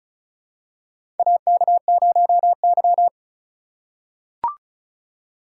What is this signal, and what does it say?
Morse code, heard as a keyed tone, about 36 wpm, 700 Hz. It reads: AK0Y
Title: USA Call Signs - 4 Characters - 2x Character Spacing - 35wpm